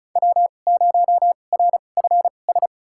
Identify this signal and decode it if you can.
Morse code, heard as a keyed tone, about 36 wpm, spaced as for 25 wpm, 700 Hz. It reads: W0RFS
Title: USA Call Signs - 2x Character Spacing - 35wpm